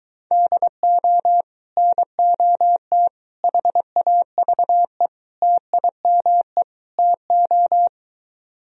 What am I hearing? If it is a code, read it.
Morse code, heard as a keyed tone, about 24 wpm, 700 Hz. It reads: DO NOT HAVE TIME TO